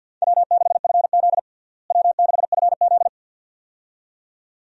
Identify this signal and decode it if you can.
Morse code, heard as a keyed tone, about 50 wpm, 700 Hz. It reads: W6FZ W6FZ
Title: USA Call Signs - 4 Characters - Repeated 2x - 50wpm